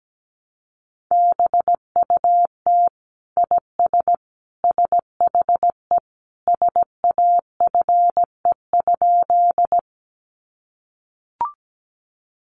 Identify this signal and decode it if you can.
Morse code, heard as a keyed tone, about 17 wpm, 700 Hz. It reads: BUT IS SHE SAFE?
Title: Sentences from Top 900 Words 17wpm